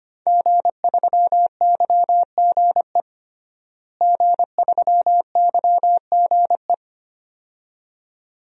Morse code, heard as a keyed tone, about 25 wpm, 700 Hz. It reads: G3YGE G3YGE
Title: DXCC-100 Call Signs - 5 Characters - Repeated 2x - 25wpm